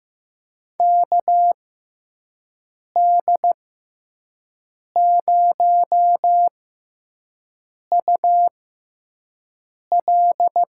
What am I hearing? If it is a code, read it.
Morse code, heard as a keyed tone, about 15 wpm, spaced as for 5 wpm, 700 Hz. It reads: KD0UL
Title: USA Call Signs - 6 Characters - 6x Character Spacing - 15wpm